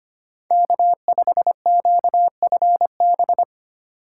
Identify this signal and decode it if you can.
Morse code, heard as a keyed tone, about 31 wpm, 700 Hz. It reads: K5QFB
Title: Speed Racing - USA Call Signs 25wpm